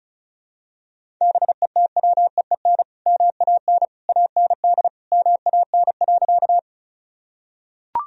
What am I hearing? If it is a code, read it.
Morse code, heard as a keyed tone, about 36 wpm, 700 Hz. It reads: BETWEEN MAN AND MAN.
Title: Sentences from Top 1000 Words - Review Part 1 - 35wpm